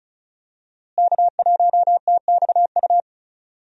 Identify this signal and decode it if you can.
Morse code, heard as a keyed tone, about 36 wpm, 700 Hz. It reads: K1TXU
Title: USA Call Signs - 5 Characters - Repeated 2x - 35wpm